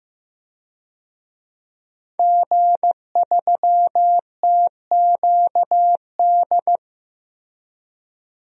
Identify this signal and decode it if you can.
Morse code, heard as a keyed tone, about 15 wpm, 700 Hz. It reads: G3TQD
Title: DXCC-100 Call Signs - 5 Characters - Repeated 3x - 15wpm